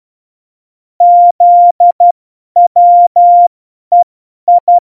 Morse code, heard as a keyed tone, about 12 wpm, spaced as for 10 wpm, 700 Hz. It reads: ZWEI
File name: Morsekurs / Witze / Witz 20. November 2018, 50 F70